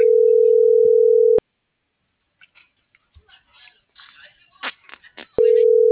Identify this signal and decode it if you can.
Ringback tone (North America), 440 + 480 Hz.